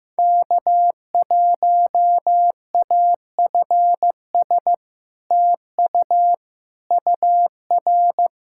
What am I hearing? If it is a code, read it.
Morse code, heard as a keyed tone, about 16 wpm, 700 Hz. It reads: K1AFS TU UR
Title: SOTA QSO Practice 15wpm